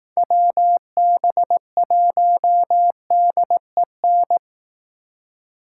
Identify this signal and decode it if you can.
Morse code, heard as a keyed tone, about 18 wpm, 700 Hz. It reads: WB1DEN
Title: Speed Racing - USA Call Signs 15wpm